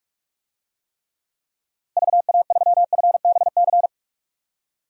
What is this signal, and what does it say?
Morse code, heard as a keyed tone, about 46 wpm, 700 Hz. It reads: VA3FBC
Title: DXCC-100 Call Signs - 6 Characters - Repeated 2x - 45wpm